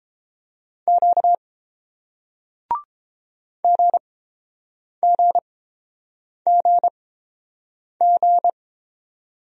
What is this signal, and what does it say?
Morse code, heard as a keyed tone, about 29 wpm, spaced as for 7 wpm, 700 Hz. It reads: Q GGGG